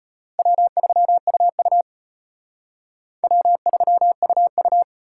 Morse code, heard as a keyed tone, about 33 wpm, 700 Hz. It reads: W3UU W3UU